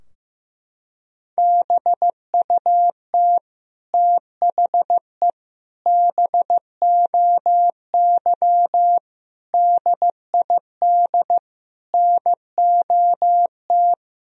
Morse code, heard as a keyed tone, about 15 wpm, 700 Hz. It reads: BUT THE BOY DID NOT
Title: Sentences from Top 1000 Words - Review Part 3 - 15wpm